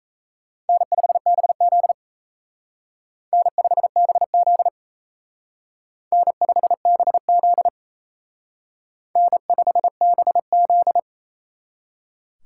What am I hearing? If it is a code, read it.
Morse code, heard as a keyed tone, about 35 wpm, 700 Hz. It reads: N5BZ N5BZ N5BZ N5BZ